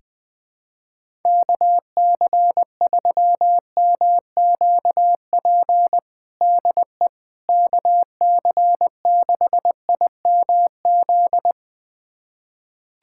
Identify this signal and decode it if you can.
Morse code, heard as a keyed tone, about 20 wpm, 700 Hz. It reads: KC3MQP DE KC6IMZ
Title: USA Callsign DE Callsign 20wpm